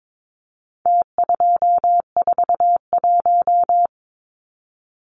Morse code, heard as a keyed tone, about 22 wpm, 700 Hz. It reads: T241
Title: Numbers - Contest Style - 4-Digit 22wpm